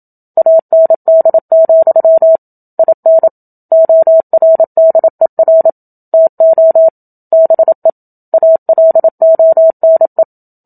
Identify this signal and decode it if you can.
Morse code, heard as a keyed tone, about 27 wpm, 650 Hz. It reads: AND, IN ORDER TO BE ALONE
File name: Book Two / 01-Chapter 1 - 27 WPM